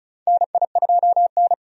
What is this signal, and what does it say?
Morse code, heard as a keyed tone, about 36 wpm, 700 Hz. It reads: NI2N
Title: USA Call Signs - 4 Characters - 35wpm